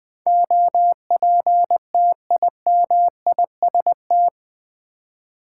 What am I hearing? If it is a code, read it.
Morse code, heard as a keyed tone, about 20 wpm, 700 Hz. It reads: OPTIMIST
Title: Build A Word - 4 Letters At A Time - 10-Letter Words - 20wpm